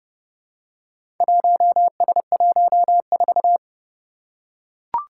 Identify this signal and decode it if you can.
Morse code, heard as a keyed tone, about 30 wpm, 700 Hz. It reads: 1S14
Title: Four Challenging Letter-Number Combinations 30wpm